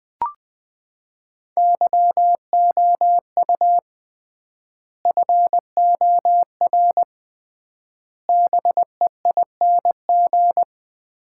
Morse code, heard as a keyed tone, about 19 wpm, 700 Hz. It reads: YOU FOR BEING
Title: 3x Word-Spacing - Sets of 5 Words - 20wpm